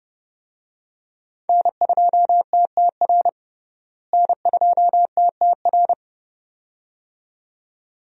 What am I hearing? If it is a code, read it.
Morse code, heard as a keyed tone, about 30 wpm, 700 Hz. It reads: N2TTR N2TTR
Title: USA Call Signs - 5 Characters - Repeated 2x - 30wpm